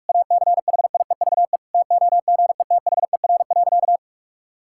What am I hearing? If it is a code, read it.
Morse code, heard as a keyed tone, about 46 wpm, 700 Hz. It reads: ACHIEVE TOGETHER.
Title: Sentences from Top 1000 Words - Review Part 6 - 45wpm